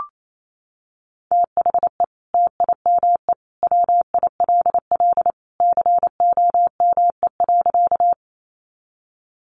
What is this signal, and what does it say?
Morse code, heard as a keyed tone, about 28 wpm, 700 Hz. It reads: THE TIME WILL COME.
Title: Sentences from Top 100 Words 28wpm